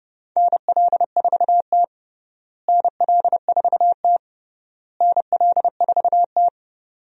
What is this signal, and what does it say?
Morse code, heard as a keyed tone, about 32 wpm, 700 Hz. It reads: NL4T NL4T NL4T
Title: USA Call Signs - 4 Characters - Repeated 3x - 30wpm